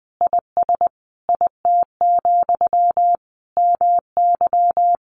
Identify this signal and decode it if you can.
Morse code, heard as a keyed tone, about 19 wpm, 700 Hz. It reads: IS IT, MY
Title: Sentences from Top 1100 Words 20wpm